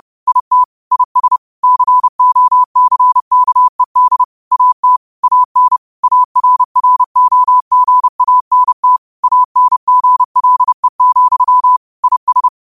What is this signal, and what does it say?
Morse code, heard as a keyed tone, about 30 wpm, 1000 Hz. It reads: IT IS COCKED AT AN ARROGANT ANGLE, IS
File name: qotdfiles / 30wpm